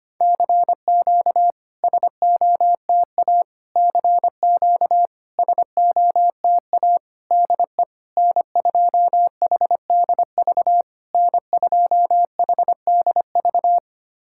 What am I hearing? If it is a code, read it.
Morse code, heard as a keyed tone, about 25 wpm, 690 Hz. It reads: CQ SOTA CQ SOTA DE N2HDV N2HDV
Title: SOTA QSO Practice 25wpm